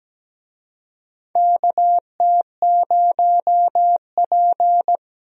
Morse code, heard as a keyed tone, about 17 wpm, 700 Hz. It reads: KT0P
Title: USA Call Signs - 4 Characters - Repeated 3x - 17wpm